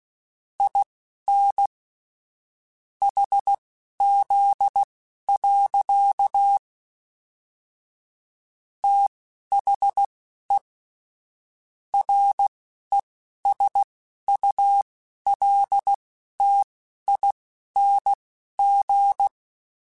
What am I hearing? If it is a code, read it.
Morse code, heard as a keyed tone, about 16 wpm, spaced as for 11 wpm, 800 Hz. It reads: IN HZ. THE RESULTING
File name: Technical Subjects / Technical Subjects-mp3s / XL @10